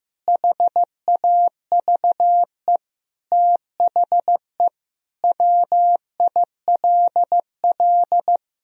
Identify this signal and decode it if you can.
Morse code, heard as a keyed tone, about 14 wpm, 700 Hz. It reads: HAVE THE WILL
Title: Sentences from Top 1000 Words - Review Part 1 - 15wpm